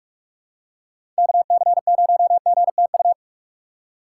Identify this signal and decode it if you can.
Morse code, heard as a keyed tone, about 42 wpm, 700 Hz. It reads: KC0GTU
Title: USA Call Signs - 6x Character Spacing - 45wpm